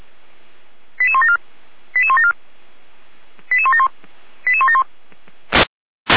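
Five-tone selective call (ZVEI), heard as quick runs of five tones, tones of about 80 ms. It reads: 88174 88174 88171 88171
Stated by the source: ZVEI five-tone call received by SDR, from the feuerwehr-orlamuende.org archive.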